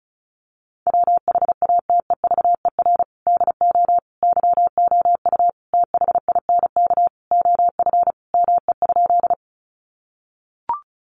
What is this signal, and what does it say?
Morse code, heard as a keyed tone, about 34 wpm, 700 Hz. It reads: WHATEVER DO YOU THINK OF ME?
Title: Sentences from Top 1100 Words 35wpm